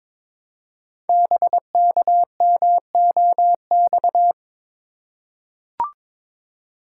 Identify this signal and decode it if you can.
Morse code, heard as a keyed tone, about 22 wpm, 700 Hz. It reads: BKMOX
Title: Code Groups - 5 Letters - 22wpm